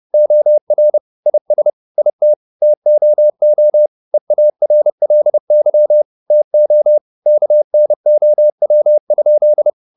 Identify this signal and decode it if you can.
Morse code, heard as a keyed tone, about 30 wpm, 600 Hz. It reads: OR IS IT TOO EARLY TO KNOW?
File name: the old man and the sea / the old man and the sea 1 30wpm